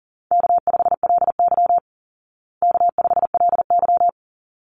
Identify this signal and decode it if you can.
Morse code, heard as a keyed tone, about 40 wpm, 700 Hz. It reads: K5LY K5LY
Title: Easy Call Signs Repeated 2x - 40wpm